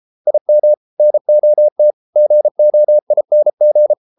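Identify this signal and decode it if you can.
Morse code, heard as a keyed tone, about 33 wpm, 600 Hz. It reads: IM NOT GOING
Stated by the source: Morse code practice at 32 wpm from CW med Gnister, reading Ken Follett's novel Never in English.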